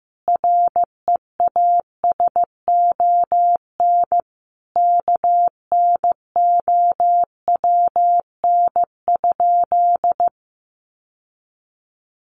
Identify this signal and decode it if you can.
Morse code, heard as a keyed tone, about 15 wpm, 700 Hz. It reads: REASON KNOWN?
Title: Sentences from Top 700 Words 15wpm